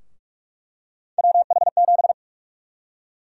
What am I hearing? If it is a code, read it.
Morse code, heard as a keyed tone, about 42 wpm, 700 Hz. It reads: WH7
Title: USA Call Sign Prefixes - Repeated 2x - 45wpm